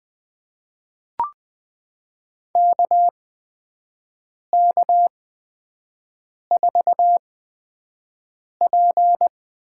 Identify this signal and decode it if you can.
Morse code, heard as a keyed tone, about 21 wpm, spaced as for 5 wpm, 700 Hz. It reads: KK4P
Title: USA Call Signs - 8x Character Spacing - 20wpm